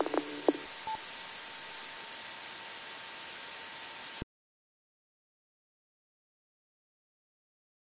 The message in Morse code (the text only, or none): none